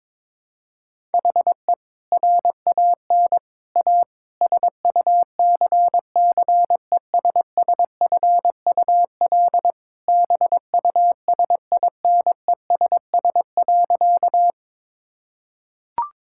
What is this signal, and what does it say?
Morse code, heard as a keyed tone, about 22 wpm, 700 Hz. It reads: HE RAN A SUCCESSFUL BUSINESS.
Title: Sentences from Top 1000 Words - Review Part 6 - 22wpm